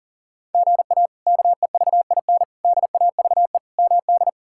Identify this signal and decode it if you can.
Morse code, heard as a keyed tone, about 40 wpm, 700 Hz. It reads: GA KEVIN DAVE MD
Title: K1USN SST QSO Practice - 40wpm